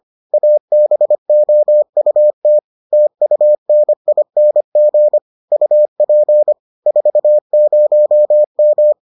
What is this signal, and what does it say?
Morse code, heard as a keyed tone, about 25 wpm, 600 Hz. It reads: ABOUT TUNING UP 40M